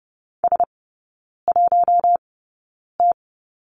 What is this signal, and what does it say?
Morse code, heard as a keyed tone, about 30 wpm, spaced as for 9 wpm, 700 Hz. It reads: S1T
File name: Lesson 16e - Letter c - Three Characters 30wpm